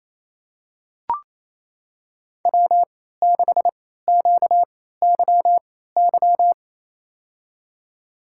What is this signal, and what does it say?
Morse code, heard as a keyed tone, about 28 wpm, spaced as for 16 wpm, 700 Hz. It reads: W6QYY
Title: USA Call Signs - 3x Character Spacing - 28wpm